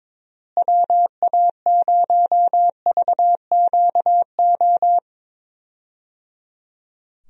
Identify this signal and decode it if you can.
Morse code, heard as a keyed tone, about 22 wpm, 700 Hz. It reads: WA0VQO
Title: USA Call Signs - 6 Characters - Repeated 3x - 22wpm